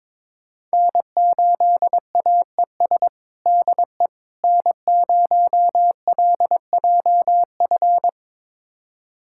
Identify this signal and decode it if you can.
Morse code, heard as a keyed tone, about 22 wpm, 700 Hz. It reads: N8AES DE N0LJF
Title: USA Callsign DE Callsign - 5 Characters - 22wpm